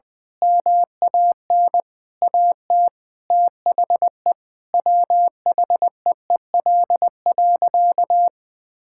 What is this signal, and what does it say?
Morse code, heard as a keyed tone, about 21 wpm, 700 Hz. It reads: MAN AT THE WHEEL.
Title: Sentences from Top 1000 Words - Review Part 5 - 20wpm